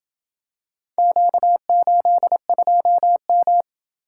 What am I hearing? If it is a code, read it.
Morse code, heard as a keyed tone, about 29 wpm, 700 Hz. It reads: Q82M